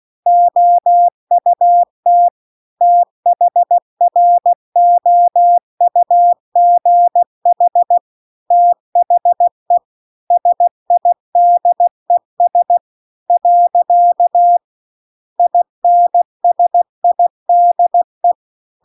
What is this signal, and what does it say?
Morse code, heard as a keyed tone, about 16 wpm, 700 Hz. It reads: OUT THROUGH THE SIDES . INSIDE